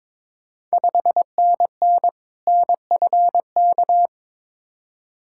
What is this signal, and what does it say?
Morse code, heard as a keyed tone, about 22 wpm, 700 Hz. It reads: 5NN NFK
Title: Ontario QSO Party Exchange 22wpm